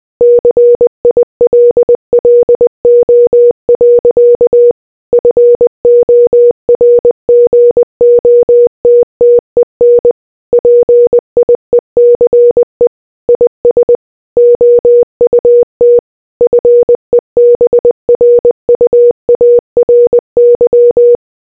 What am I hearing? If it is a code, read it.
Morse code, heard as a keyed tone, about 20 wpm, 480 Hz. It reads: CILLO. FORGOTTEN PIECE IS OUT FEBRUARY